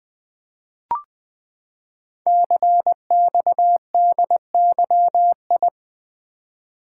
Morse code, heard as a keyed tone, about 20 wpm, 700 Hz. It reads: CXDYI
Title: Code Groups - 5 Letters - 20wpm